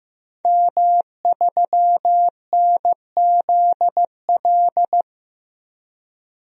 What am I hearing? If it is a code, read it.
Morse code, heard as a keyed tone, about 15 wpm, 700 Hz. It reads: M3NZL
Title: DXCC-100 Call Signs - 5 Characters - 15wpm